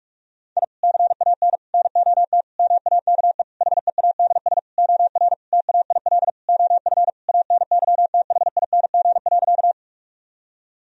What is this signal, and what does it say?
Morse code, heard as a keyed tone, about 46 wpm, 700 Hz. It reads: I CAN NOT MAKE HEADS OR TAIL OF ANYTHING.
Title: Sentences from Top 1000 Words - Review Part 3 - 45wpm